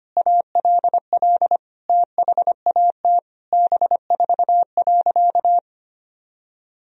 Morse code, heard as a keyed tone, about 25 wpm, 700 Hz. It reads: ALL THAT B4.
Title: Sentences from Top 400 Words and Abbreviations 25wpm